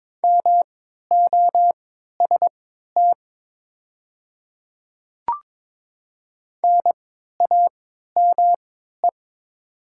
Morse code, heard as a keyed tone, about 22 wpm, spaced as for 13 wpm, 690 Hz. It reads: MOST NAME
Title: 3x Character Spacing - Top 200 Words - 22wpm